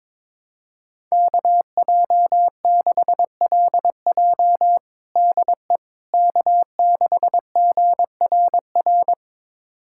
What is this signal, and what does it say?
Morse code, heard as a keyed tone, about 22 wpm, 700 Hz. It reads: KJ6LJ DE K6GRR
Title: USA Callsign DE Callsign - 5 Characters - 22wpm